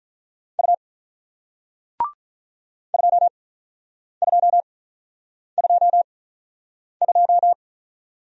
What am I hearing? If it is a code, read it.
Morse code, heard as a keyed tone, about 48 wpm, spaced as for 9 wpm, 700 Hz. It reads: U 2222